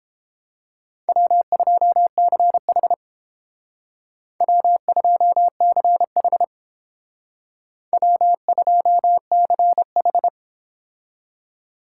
Morse code, heard as a keyed tone, about 31 wpm, 700 Hz. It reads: W2CH W2CH W2CH